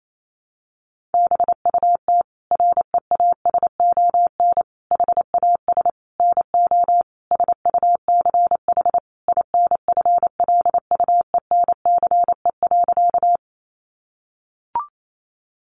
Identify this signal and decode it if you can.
Morse code, heard as a keyed tone, about 28 wpm, 700 Hz. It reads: BUT REASON HAS NO SUCH INFLUENCE.
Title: Sentences from Top 1100 Words 28wpm